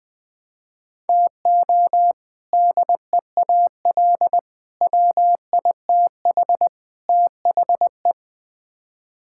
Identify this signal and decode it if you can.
Morse code, heard as a keyed tone, about 20 wpm, 700 Hz. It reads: TO DEAL WITH THE
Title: Build A Phrase - 4-Word Phrases - 20wpm